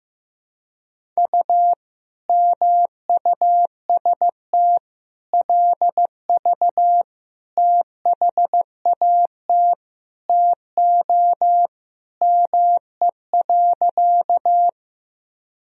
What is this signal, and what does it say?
Morse code, heard as a keyed tone, about 16 wpm, 700 Hz. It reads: U MUST LV THAT TO ME.
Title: Sentences from Top 400 Words and Abbreviations 15wpm